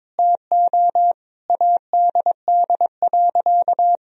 Morse code, heard as a keyed tone, about 22 wpm, 700 Hz. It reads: TO ADD.
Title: Sentences from Top 1000 Words - Review Part 1 - 22wpm